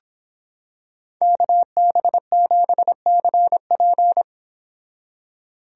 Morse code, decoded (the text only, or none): KB7CP